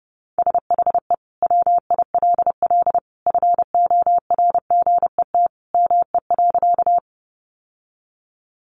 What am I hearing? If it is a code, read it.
Morse code, heard as a keyed tone, about 30 wpm, 700 Hz. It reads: SHE WILL FORGET ME.
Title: Sentences from Top 1200 Words 30wpm